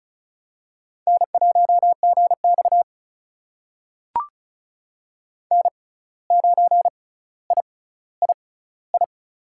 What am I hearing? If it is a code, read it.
Morse code, heard as a keyed tone, about 34 wpm, spaced as for 17 wpm, 700 Hz. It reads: N1GX N9III
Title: USA Call Signs - 6x Character Spacing - 35wpm